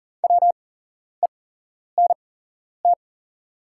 Morse code, heard as a keyed tone, about 40 wpm, spaced as for 11 wpm, 690 Hz. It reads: WENT
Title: 8x Character Spacing - Top 200 Words - 40wpm